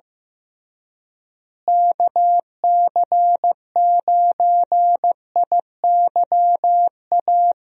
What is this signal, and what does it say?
Morse code, heard as a keyed tone, about 15 wpm, 700 Hz. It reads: KC9IYA
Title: USA Call Signs - 6 Characters - Repeated 3x - 15wpm